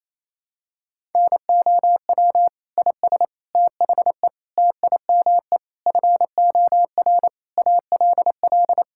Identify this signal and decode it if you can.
Morse code, heard as a keyed tone, about 28 wpm, 700 Hz. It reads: NOW IS THE TIME FOR ALL
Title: Pangram Challenge v2 - 28wpm